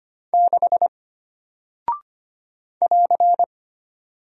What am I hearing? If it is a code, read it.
Morse code, heard as a keyed tone, about 25 wpm, 700 Hz. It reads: 6 <AR>
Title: Single Character (Letter:Number:Punctuation:Prosign) Alt 25wpm